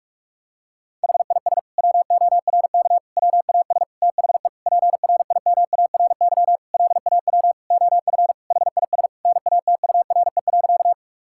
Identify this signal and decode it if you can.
Morse code, heard as a keyed tone, about 46 wpm, 690 Hz. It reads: HIS WORK WAS THE PRIMARY LAW OF HIS NATURE.
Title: Sentences from Top 1000 Words - Review Part 5 - 45wpm